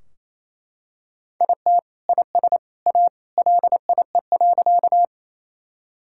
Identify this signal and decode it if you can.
Morse code, heard as a keyed tone, about 28 wpm, 700 Hz. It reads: IT IS A LIE.
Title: Sentences from Top 1600 Words 28wpm